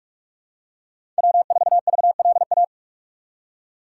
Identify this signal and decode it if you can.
Morse code, heard as a keyed tone, about 43 wpm, 700 Hz. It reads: W4VLA
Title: USA Call Signs - 5 Characters - 6x Character Spacing - 45wpm